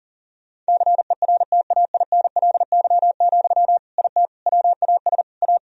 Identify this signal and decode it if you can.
Morse code, heard as a keyed tone, about 40 wpm, 700 Hz. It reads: CERTAINLY, IT WAS A
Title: Sentences from Top 1000 Words - Review Part 2 - 40wpm